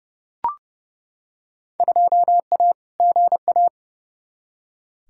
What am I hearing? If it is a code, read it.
Morse code, heard as a keyed tone, about 31 wpm, 700 Hz. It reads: 2A GA
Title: Field Day Exchanges Encore - 30wpm